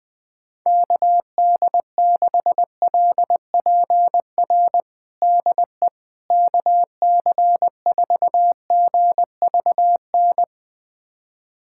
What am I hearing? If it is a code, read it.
Morse code, heard as a keyed tone, about 20 wpm, 700 Hz. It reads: KD6LPR DE KC4GVN
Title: USA Callsign DE Callsign - 6 Characters - 20wpm